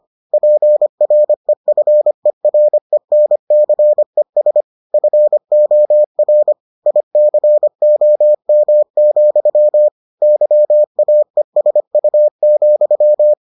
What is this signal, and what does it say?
Morse code, heard as a keyed tone, about 25 wpm, 600 Hz. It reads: PREFERENCES FOR ICOM, YAESU,